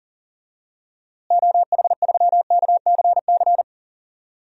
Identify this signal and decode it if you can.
Morse code, heard as a keyed tone, about 40 wpm, 700 Hz. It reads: OH3KCC
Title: DXCC-100 Call Signs - 6 Characters - Repeated 3x - 40wpm